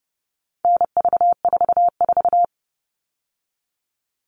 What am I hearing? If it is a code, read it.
Morse code, heard as a keyed tone, about 30 wpm, 700 Hz. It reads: NV44
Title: All Cut Numbers - 4-Digit 30wpm